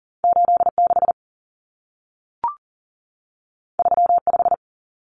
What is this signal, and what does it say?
Morse code, heard as a keyed tone, about 40 wpm, 700 Hz. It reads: 86 35
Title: Common Cut Numbers - 2-Digit 40wpm